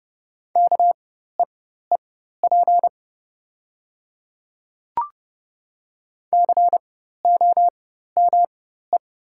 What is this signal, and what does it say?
Morse code, heard as a keyed tone, about 29 wpm, spaced as for 14 wpm, 700 Hz. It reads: KEEP COME
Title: 4x Character Spacing - Top 300 Words - 30wpm